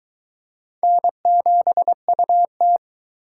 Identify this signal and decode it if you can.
Morse code, heard as a keyed tone, about 22 wpm, 700 Hz. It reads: N7UT